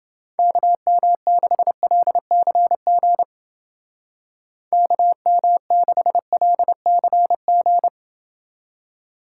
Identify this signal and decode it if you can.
Morse code, heard as a keyed tone, about 29 wpm, 700 Hz. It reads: KM6LCG KM6LCG